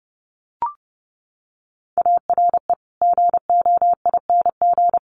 Morse code, heard as a keyed tone, about 30 wpm, 700 Hz. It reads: ARE GOING
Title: Sets of 5 Words Encore2 30wpm